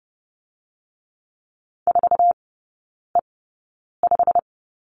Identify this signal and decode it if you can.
Morse code, heard as a keyed tone, about 30 wpm, spaced as for 9 wpm, 700 Hz. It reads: 4E5
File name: course / 30:10wpm / 087 Lesson 15e - Number 5 - Three Characters 30wpm